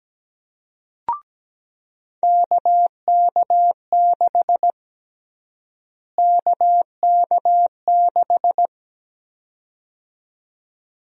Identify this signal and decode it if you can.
Morse code, heard as a keyed tone, about 17 wpm, 700 Hz. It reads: KK6 KK6
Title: USA Call Sign Prefixes - Repeated 2x - 17wpm